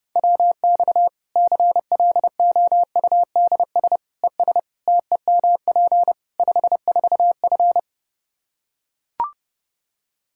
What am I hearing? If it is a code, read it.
Morse code, heard as a keyed tone, about 30 wpm, 700 Hz. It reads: WX CLOUDS ES TEMP 54F
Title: Practice QSO - Exchange 2 - Info - 30wpm